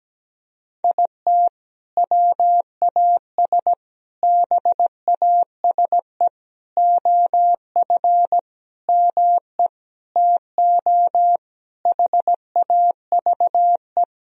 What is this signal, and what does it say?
Morse code, heard as a keyed tone, about 17 wpm, 700 Hz. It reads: IT WAS BASE OF ME TO HAVE
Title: Sentences from Top 1000 Words - Review Part 1 - 17wpm